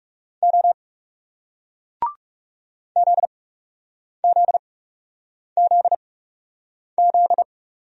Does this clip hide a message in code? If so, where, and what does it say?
Yes: Morse code from 0.4 to 7.4 s, heard as a keyed tone, about 38 wpm, spaced as for 8 wpm, 700 Hz. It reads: O ZZZZ